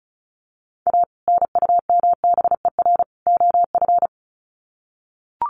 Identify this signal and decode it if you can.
Morse code, heard as a keyed tone, about 34 wpm, 700 Hz. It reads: A NUMBER OF
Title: Sets of 3 Words 35wpm